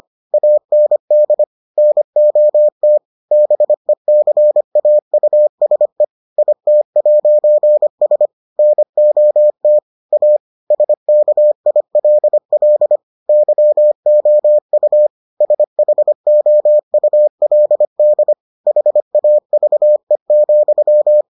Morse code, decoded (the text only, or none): AND NOT BECAUSE IT'S NOT A SKILL YOU SHOULD HAVE,